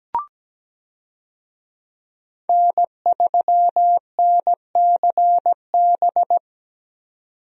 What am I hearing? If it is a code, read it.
Morse code, heard as a keyed tone, about 17 wpm, 700 Hz. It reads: N3NCB
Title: USA Call Signs - 5 Characters - 17wpm